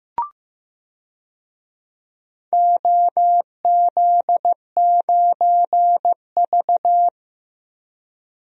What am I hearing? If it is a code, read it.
Morse code, heard as a keyed tone, about 15 wpm, 700 Hz. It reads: OZ9V
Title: DXCC-100 Call Signs - 4 Characters - Repeated 2x - 15wpm